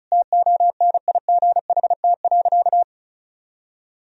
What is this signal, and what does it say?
Morse code, heard as a keyed tone, about 36 wpm, 700 Hz. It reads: TONIGHT.
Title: Sentences from Top 1000 Words - Review Part 4 - 35wpm